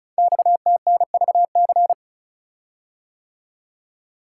Morse code, heard as a keyed tone, about 36 wpm, 690 Hz. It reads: XTNVC
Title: Code Groups - 5 Letter:Numbers - 35wpm